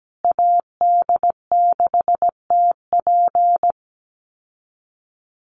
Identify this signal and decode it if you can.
Morse code, heard as a keyed tone, about 17 wpm, 700 Hz. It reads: AD6TP
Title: USA Call Signs Repeated 3x 17wpm